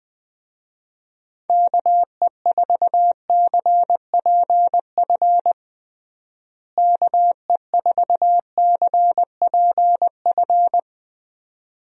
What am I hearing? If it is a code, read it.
Morse code, heard as a keyed tone, about 21 wpm, 700 Hz. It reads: KE4CPF KE4CPF
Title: USA Call Signs - 6 Characters - Repeated 2x - 20wpm